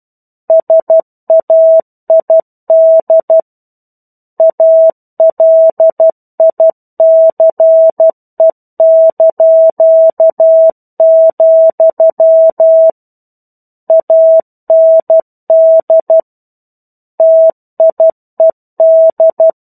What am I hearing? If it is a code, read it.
Morse code, heard as a keyed tone, about 12 wpm, 650 Hz. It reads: SAID ALICE), AND TIED
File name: 01-Chapter-01 12 WPM